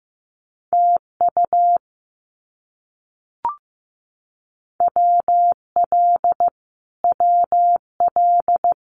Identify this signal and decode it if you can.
Morse code, heard as a keyed tone, about 15 wpm, 700 Hz. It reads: TU WL WL
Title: Abbreviations Repeated 3x - 15wpm